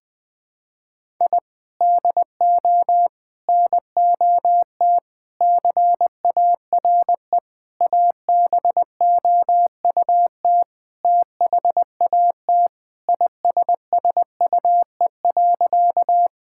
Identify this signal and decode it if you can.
Morse code, heard as a keyed tone, about 20 wpm, 700 Hz. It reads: I DO NOT CARE ABOUT THAT ISSUE.
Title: Sentences from Top 1000 Words - Review Part 6 - 20wpm